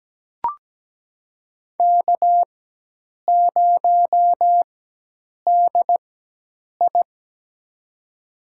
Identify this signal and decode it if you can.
Morse code, heard as a keyed tone, about 17 wpm, spaced as for 8 wpm, 700 Hz. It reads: K0DI
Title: USA Call Signs - 4 Characters - 4x Character Spacing - 17wpm